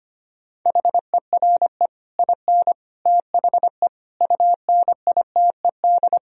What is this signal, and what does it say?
Morse code, heard as a keyed tone, about 26 wpm, 700 Hz. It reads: HERE IN THE UNITED
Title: Speed-Racing - Sets of 5 Words 25wpm